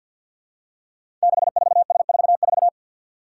Morse code, heard as a keyed tone, about 46 wpm, 700 Hz. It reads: 64S44
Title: Five Challenging Letter-Number Combinations 50wpm